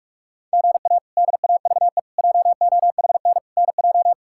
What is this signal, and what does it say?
Morse code, heard as a keyed tone, about 46 wpm, 700 Hz. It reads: GA DAVE JOHN NJ
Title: K1USN SST QSO Practice - 45wpm